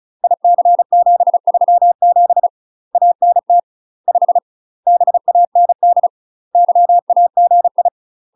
Morse code, heard as a keyed tone, about 36 wpm, 700 Hz. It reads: IC737 ANT 5 BAND YAGI
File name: UK MASTER QSOs TRAINING / 35WPM / UK MASTER QSOs 0010